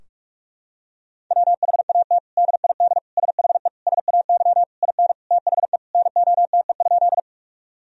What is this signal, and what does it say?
Morse code, heard as a keyed tone, about 41 wpm, 700 Hz. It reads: WHAT DID SHE SAY IN THE NOTE?
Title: Sentences from Top 1000 Words - Review Part 2 - 45wpm